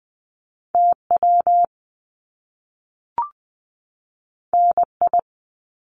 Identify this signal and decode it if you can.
Morse code, heard as a keyed tone, about 20 wpm, 700 Hz. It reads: TW NI
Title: Top Two Letter Combinations in Words 20wpm